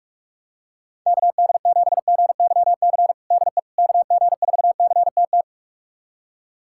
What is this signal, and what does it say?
Morse code, heard as a keyed tone, about 45 wpm, 700 Hz. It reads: KD7GYC DE KG4CTT
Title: USA Callsign DE Callsign - 6 Characters - 45wpm